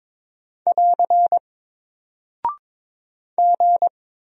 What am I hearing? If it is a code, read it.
Morse code, heard as a keyed tone, about 22 wpm, 700 Hz. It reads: <AR> G
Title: Single Character (Letter:Number:Punctuation:Prosign) Alt 22wpm